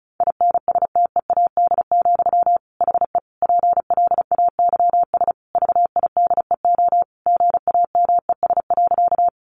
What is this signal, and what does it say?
Morse code, heard as a keyed tone, about 34 wpm, 700 Hz. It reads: INSTEAD, HE PLAYS VIDEO GAMES.
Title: Additional Easy Phrases 35wpm